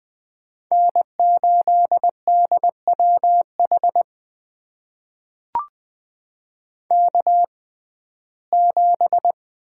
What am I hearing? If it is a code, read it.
Morse code, heard as a keyed tone, about 20 wpm, spaced as for 14 wpm, 700 Hz. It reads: N8DWH K7
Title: USA Call Signs - 5 Characters - 6x Character Spacing - 20wpm